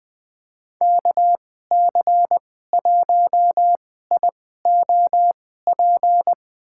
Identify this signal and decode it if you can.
Morse code, heard as a keyed tone, about 20 wpm, spaced as for 15 wpm, 700 Hz. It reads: KC1IOP
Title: USA Call Signs - 6 Characters - 2x Character Spacing - 20wpm